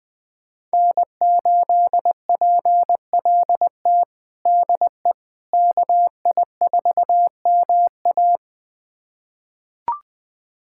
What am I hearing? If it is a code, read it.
Morse code, heard as a keyed tone, about 20 wpm, 700 Hz. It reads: N8PLT DE KI4MA
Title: USA Callsign DE Callsign - 5 Characters - 20wpm